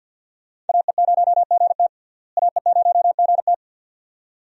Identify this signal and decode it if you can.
Morse code, heard as a keyed tone, about 47 wpm, 700 Hz. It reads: AE0GT AE0GT
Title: USA Call Signs - 5 Characters - Repeated 2x - 50wpm